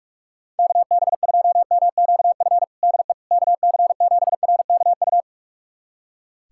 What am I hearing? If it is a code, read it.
Morse code, heard as a keyed tone, about 42 wpm, 690 Hz. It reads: KB2MQF DE KC7RKU
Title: USA Callsign DE Callsign 45wpm